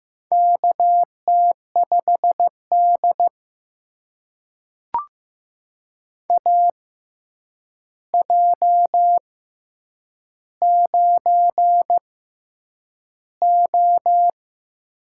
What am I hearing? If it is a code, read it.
Morse code, heard as a keyed tone, about 15 wpm, spaced as for 8 wpm, 700 Hz. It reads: KT5D AJ9O
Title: USA Call Signs - 4 Characters - 6x Character Spacing - 15wpm